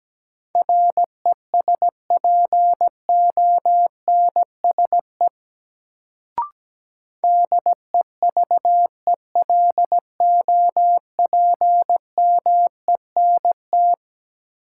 Morse code, heard as a keyed tone, about 17 wpm, 700 Hz. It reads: RESPONSE DEVELOPMENT
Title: Sentences from Top 1400 Words - New Word Drill Alt - 17wpm